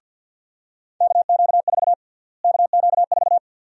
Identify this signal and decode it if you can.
Morse code, heard as a keyed tone, about 52 wpm, 700 Hz. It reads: KQ4 KQ4
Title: USA Call Sign Prefixes - Repeated 2x - 50wpm